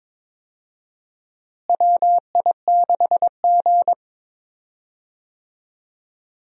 Morse code, heard as a keyed tone, about 22 wpm, 700 Hz. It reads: WI6G
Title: USA Call Signs - 4 Characters - Repeated 3x - 22wpm